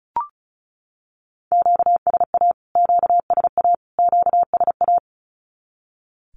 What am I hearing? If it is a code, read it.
Morse code, heard as a keyed tone, about 34 wpm, 700 Hz. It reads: QSA QSA QSA
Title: Q-Codes Repeated 3x 35wpm